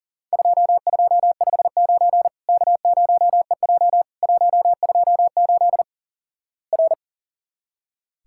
Morse code, heard as a keyed tone, about 44 wpm, 700 Hz. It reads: 2259 K0EJ 128 R
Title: Worked All Europe DX Contest - QTC Exchange - 40wpm